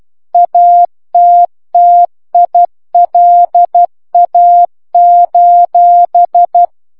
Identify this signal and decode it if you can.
Morse code, heard as a keyed tone, about 12 wpm, 700 Hz. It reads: ATTILA: